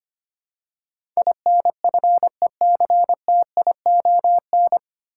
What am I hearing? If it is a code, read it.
Morse code, heard as a keyed tone, about 25 wpm, 700 Hz. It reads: INFECTION
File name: practice / 25wpm / 9 Letter Words - First 3 Letters Given - 25wpm